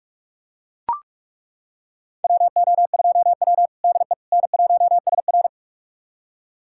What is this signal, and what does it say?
Morse code, heard as a keyed tone, about 41 wpm, 700 Hz. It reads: WO2W DE N1SR
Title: USA Callsign DE Callsign - 4 Characters - 45wpm